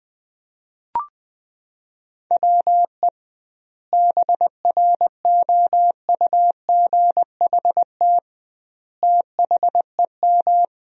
Morse code, heard as a keyed tone, about 21 wpm, 700 Hz. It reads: WE BROUGHT THEM
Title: 2x Word-Spacing - Sentences from Top 500 Words - 20wpm